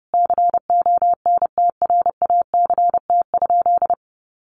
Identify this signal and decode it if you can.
Morse code, heard as a keyed tone, about 30 wpm, 700 Hz. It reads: CONTRACT?
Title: Sentences from Top 1400 Words 30wpm